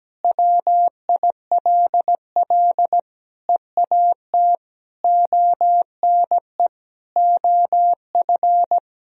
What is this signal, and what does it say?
Morse code, heard as a keyed tone, about 17 wpm, 700 Hz. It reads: WILL EAT ONE OF
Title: Sentences from Top 1000 Words - Review Part 1 - 17wpm